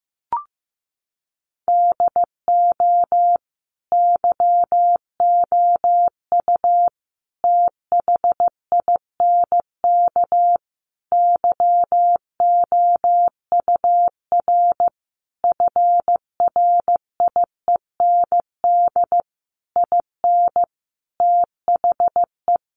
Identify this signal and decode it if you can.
Morse code, heard as a keyed tone, about 15 wpm, 700 Hz. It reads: DO YOU THINK YOUR FRIEND IN THE
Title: Sentences from Top 1100 Words 15wpm